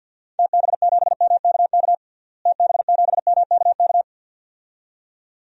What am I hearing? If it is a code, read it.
Morse code, heard as a keyed tone, about 51 wpm, 700 Hz. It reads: TB7MKK TB7MKK
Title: DXCC-100 Call Signs - 6 Characters - Repeated 2x - 50wpm